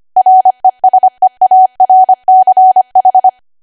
Morse code, heard as a keyed tone, about 25 wpm, 750 Hz. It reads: RESEARCH
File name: Archive / 25 WPM / 25WPM